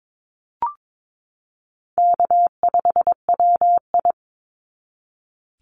Morse code, heard as a keyed tone, about 22 wpm, 700 Hz. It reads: K5WI
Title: NA SOTA Chaser Callsigns 22wpm